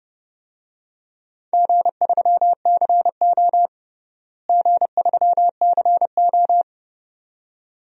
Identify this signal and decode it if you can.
Morse code, heard as a keyed tone, about 32 wpm, 700 Hz. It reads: G3CO G3CO
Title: DXCC-100 Call Signs - 4 Characters - Repeated 2x - 30wpm